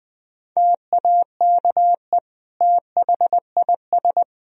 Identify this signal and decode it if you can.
Morse code, heard as a keyed tone, about 20 wpm, 700 Hz. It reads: TAKE THIS
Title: Sentences from Top 1000 Words - Review Part 4 - 20wpm